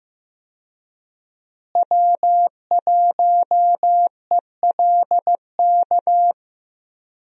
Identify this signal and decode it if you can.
Morse code, heard as a keyed tone, about 15 wpm, 700 Hz. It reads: W1ELK
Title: USA Call Signs - 5 Characters - Repeated 3x - 15wpm